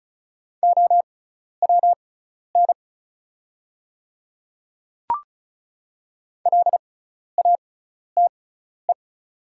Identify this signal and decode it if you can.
Morse code, heard as a keyed tone, about 36 wpm, spaced as for 12 wpm, 700 Hz. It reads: OWN LATE
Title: 6x Character Spacing - Top 300 Words - 35wpm